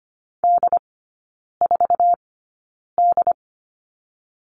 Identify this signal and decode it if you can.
Morse code, heard as a keyed tone, about 25 wpm, spaced as for 9 wpm, 690 Hz. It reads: D4D
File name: Lesson 14e - Number 2 - Three Characters 25wpm